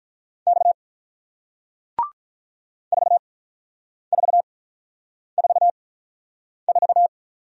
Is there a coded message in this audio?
Yes: Morse code from 0.5 to 7.1 s, heard as a keyed tone, about 48 wpm, spaced as for 9 wpm, 700 Hz. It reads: X 4444